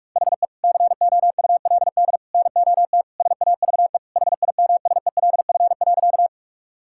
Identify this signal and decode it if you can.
Morse code, heard as a keyed tone, about 46 wpm, 700 Hz. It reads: HE COULD NOT SAVE HIMSELF.
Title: Sentences from Top 1600 Words 45wpm